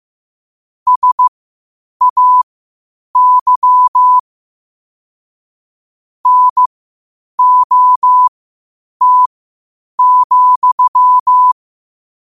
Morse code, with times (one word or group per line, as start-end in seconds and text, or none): 0.9-4.2 SAY
6.3-11.5 NOT,